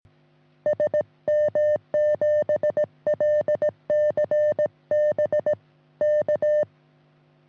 Morse code, heard as a keyed tone, about 17 wpm, 600 Hz. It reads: SM7LCB K
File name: sm7lcb / sm7ean 10g-24g